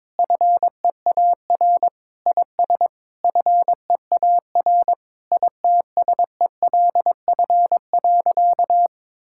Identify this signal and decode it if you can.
Morse code, heard as a keyed tone, about 22 wpm, 700 Hz. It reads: FEAR IS FEAR ITSELF.
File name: practice / 22wpm / Sentences with 8-Word Phrases - 22wpm